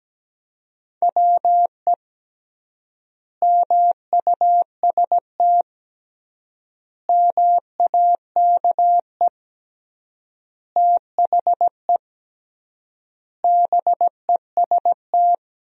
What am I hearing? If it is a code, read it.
Morse code, heard as a keyed tone, about 17 wpm, 700 Hz. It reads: WE MUST MAKE THE BEST
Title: 3x Word-Spacing - Sentences from Top 400 Words - 17wpm